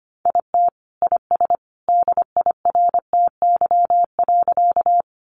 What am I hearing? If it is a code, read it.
Morse code, heard as a keyed tone, about 25 wpm, 700 Hz. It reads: IT IS DIRTY.
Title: Additional Easy Phrases 25wpm